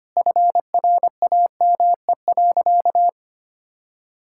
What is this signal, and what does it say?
Morse code, heard as a keyed tone, about 25 wpm, 700 Hz. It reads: FRAME.
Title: Sentences from Top 1000 Words - Review Part 6 - 25wpm